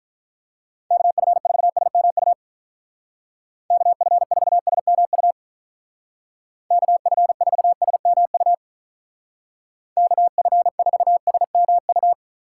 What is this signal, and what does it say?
Morse code, heard as a keyed tone, about 44 wpm, 700 Hz. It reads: KF4SMU KF4SMU KF4SMU KF4SMU